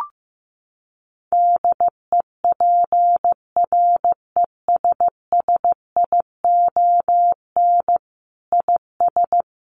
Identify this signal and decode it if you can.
Morse code, heard as a keyed tone, about 15 wpm, 700 Hz. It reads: DEPRESSION IS
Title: Sentences from Top 1300 Words 15wpm